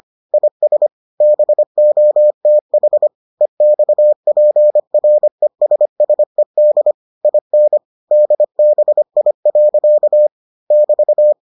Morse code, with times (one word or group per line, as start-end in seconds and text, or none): 0.3-0.9 IS
1.2-3.1 BOTH
3.4-6.9 EXPRESSED
7.2-7.8 IN
8.1-10.3 DBI.
10.7-11.3 <BT>